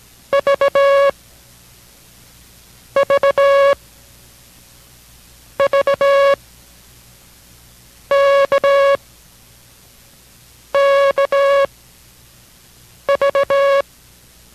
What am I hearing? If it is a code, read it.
Morse code, heard as a keyed tone, about 18 wpm, 560 Hz. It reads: V V V K K V